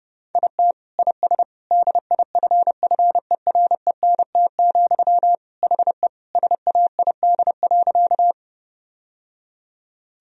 Morse code, heard as a keyed tone, about 30 wpm, 700 Hz. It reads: IT IS DIFFERENT, HE SAID.
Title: Sentences from Top 1000 Words - Review Part 3 - 30wpm